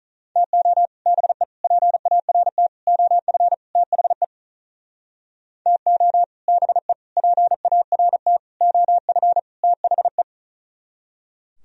Morse code, heard as a keyed tone, about 43 wpm, 700 Hz. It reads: TO BE PART OF THE TO BE PART OF THE
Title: Speed-Racing - Sets of 5 Words 35wpm